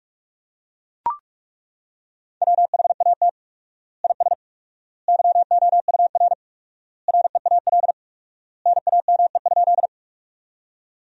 Morse code, heard as a keyed tone, about 41 wpm, 700 Hz. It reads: WHAT IS YOUR REAL NAME?
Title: 4x Word-Spacing - Sentences from Top 300 Words - 45wpm